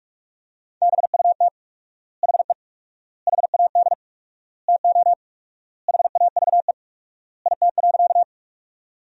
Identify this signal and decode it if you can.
Morse code, heard as a keyed tone, about 46 wpm, 700 Hz. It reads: BUT HE HAD TO HAVE IT.
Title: 4x Word-Spacing - Sentences from Top 100 Words - 45wpm